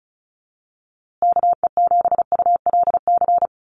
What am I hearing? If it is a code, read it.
Morse code, heard as a keyed tone, about 34 wpm, 700 Hz. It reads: KE7ULC
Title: USA Call Signs Repeated 2x 35wpm